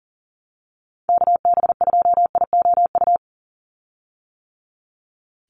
Morse code, heard as a keyed tone, about 40 wpm, 700 Hz. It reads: KB2IOU
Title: USA Call Signs 40wpm